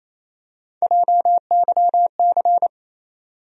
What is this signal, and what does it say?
Morse code, heard as a keyed tone, about 28 wpm, 700 Hz. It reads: JYC
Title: USA Call Sign Postfixes - 3 Letters 28wpm